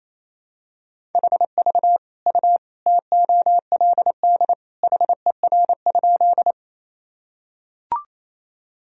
Morse code, heard as a keyed tone, about 28 wpm, 700 Hz. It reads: HV U TOLD HER?
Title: Sentences from Top 400 Words and Abbreviations 28wpm